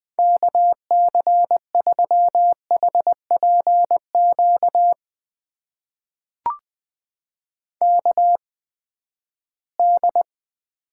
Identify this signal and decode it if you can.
Morse code, heard as a keyed tone, about 20 wpm, spaced as for 14 wpm, 700 Hz. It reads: KC3HPQ KD
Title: USA Call Signs - 6 Characters - 8x Character Spacing - 20wpm